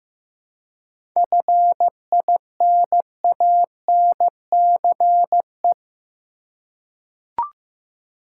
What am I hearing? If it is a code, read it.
Morse code, heard as a keyed tone, about 15 wpm, 700 Hz. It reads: FINANCE
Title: Seven Letter Words - 15wpm